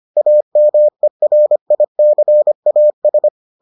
Morse code, heard as a keyed tone, about 25 wpm, 600 Hz. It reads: AMERICAS